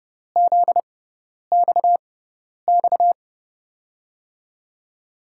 Morse code, heard as a keyed tone, about 29 wpm, spaced as for 10 wpm, 700 Hz. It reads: ZXX
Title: USA Call Signs - 6 Characters - 6x Character Spacing - 30wpm